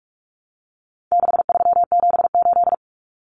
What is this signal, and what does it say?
Morse code, heard as a keyed tone, about 44 wpm, 700 Hz. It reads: 6378
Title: Numbers - 4-Digit 45wpm